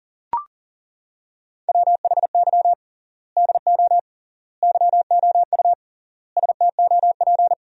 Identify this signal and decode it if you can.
Morse code, heard as a keyed tone, about 40 wpm, 700 Hz. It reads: WHY DO YOU STOP
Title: 3x Word-Spacing - Sentences from Top 300 Words - 40wpm